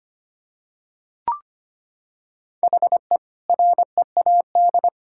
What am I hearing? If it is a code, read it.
Morse code, heard as a keyed tone, about 25 wpm, 700 Hz. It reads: HE READ
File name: practice / 25wpm / Sentences with 2-Word Phrases - 25wpm